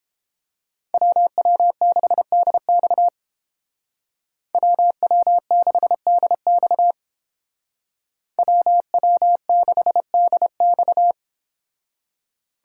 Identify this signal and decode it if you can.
Morse code, heard as a keyed tone, about 31 wpm, 700 Hz. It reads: WW6DX WW6DX WW6DX